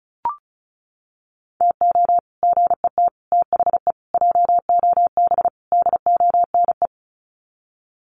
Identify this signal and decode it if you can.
Morse code, heard as a keyed tone, about 34 wpm, 700 Hz. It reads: TO GET THE JOB DONE
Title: Sets of 5 Words Encore2 35wpm